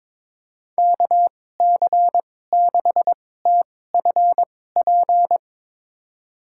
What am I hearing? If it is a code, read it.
Morse code, heard as a keyed tone, about 21 wpm, spaced as for 16 wpm, 700 Hz. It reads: KC6TFP
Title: USA Call Signs - 6 Characters - 2x Character Spacing - 22wpm